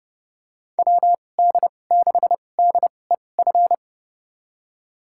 Morse code, heard as a keyed tone, about 30 wpm, spaced as for 22 wpm, 700 Hz. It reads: WD6DEF
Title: USA Call Signs - 6 Characters - 2x Character Spacing - 30wpm